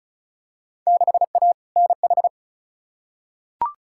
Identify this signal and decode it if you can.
Morse code, heard as a keyed tone, about 34 wpm, 700 Hz. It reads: 6A NH
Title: Field Day Exchanges - 35wpm